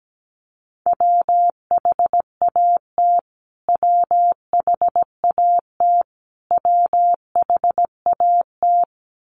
Morse code, heard as a keyed tone, about 17 wpm, 700 Hz. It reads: WHAT WHAT WHAT
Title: Top 300 Words Repeated 17wpm